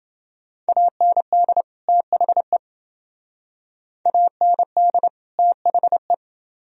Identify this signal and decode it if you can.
Morse code, heard as a keyed tone, about 28 wpm, 700 Hz. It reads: AND THE AND THE